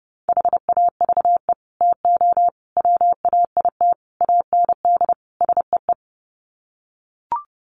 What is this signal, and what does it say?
Morse code, heard as a keyed tone, about 30 wpm, 700 Hz. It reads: HAVE TO WAIT AND SEE
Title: Sets of 5 Words Encore2 30wpm